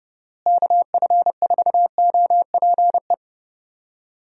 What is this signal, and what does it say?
Morse code, heard as a keyed tone, about 30 wpm, 700 Hz. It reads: KF4OPE
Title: USA Call Signs - 6 Characters - Repeated 2x - 30wpm